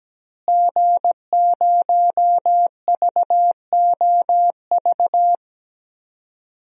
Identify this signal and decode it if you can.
Morse code, heard as a keyed tone, about 17 wpm, 700 Hz. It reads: G0VOV
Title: DXCC-100 Call Signs - 5 Characters - Repeated 2x - 17wpm